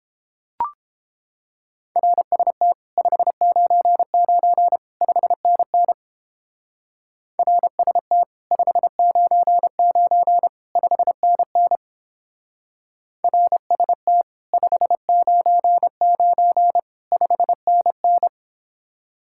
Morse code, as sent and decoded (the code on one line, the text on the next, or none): .-. ... - / ..... ----. ----. / ..... -. -. / .-. ... - / ..... ----. ----. / ..... -. -. / .-. ... - / ..... ----. ----. / ..... -. -.
RST 599 5NN RST 599 5NN RST 599 5NN